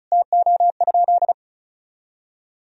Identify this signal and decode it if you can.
Morse code, heard as a keyed tone, about 36 wpm, 700 Hz. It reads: TO?
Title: Sentences from Top 1000 Words - Review Part 1 - 35wpm